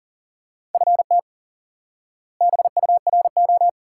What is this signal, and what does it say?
Morse code, heard as a keyed tone, about 40 wpm, 700 Hz. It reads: FT BURO
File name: IQE Course 3 - Lesson 89 - YAGI - Two Elements - 40wpm